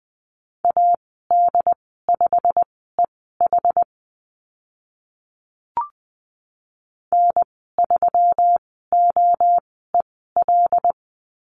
Morse code, heard as a keyed tone, about 20 wpm, spaced as for 14 wpm, 700 Hz. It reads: AD5EH N3OEL
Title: 179 Lesson 29g - Slash - Call Signs 20wpm